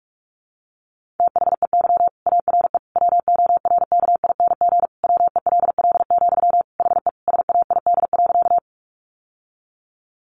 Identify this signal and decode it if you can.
Morse code, heard as a keyed tone, about 44 wpm, 700 Hz. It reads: THEY ARE WORKING WELL, HE SAID.
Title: Sentences from Top 700 Words 45wpm